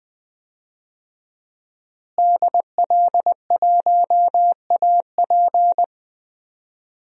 Morse code, heard as a keyed tone, about 21 wpm, 700 Hz. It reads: DL1AP
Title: DXCC-100 Call Signs - 5 Characters - Repeated 2x - 20wpm